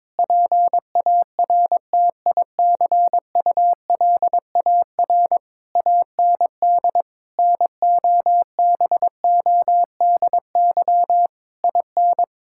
Morse code, heard as a keyed tone, about 22 wpm, 700 Hz. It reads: PARTICULAR AND NOBODY IN
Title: Sentences from Top 1600 Words 22wpm